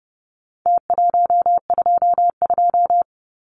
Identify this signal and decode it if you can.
Morse code, heard as a keyed tone, about 30 wpm, 700 Hz. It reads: T122
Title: Numbers - Contest Style - 4-Digit 30wpm